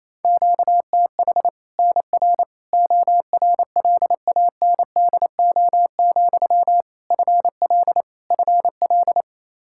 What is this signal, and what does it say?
Morse code, heard as a keyed tone, about 28 wpm, 700 Hz. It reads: QTH NR ORLANDO, FL FL
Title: Practice QSO - Exchange 1 - Beginning - 28wpm